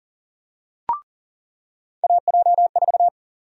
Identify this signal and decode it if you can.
Morse code, heard as a keyed tone, about 40 wpm, 700 Hz. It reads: AJ4
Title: USA Call Sign Prefixes - 40wpm